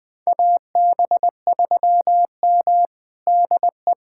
Morse code, heard as a keyed tone, about 21 wpm, 700 Hz. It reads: AB3M DE
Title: USA Callsign DE Callsign - 4 Characters - 20wpm